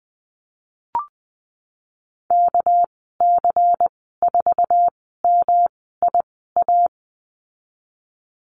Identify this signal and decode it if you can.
Morse code, heard as a keyed tone, about 20 wpm, 700 Hz. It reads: K C 4 M I A
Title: 207 Lesson 33g - Letter j - Call Signs 20wpm